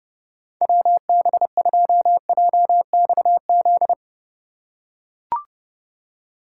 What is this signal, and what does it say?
Morse code, heard as a keyed tone, about 32 wpm, 700 Hz. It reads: WB2JXZ
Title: USA Call Signs - 6x Character Spacing - 30wpm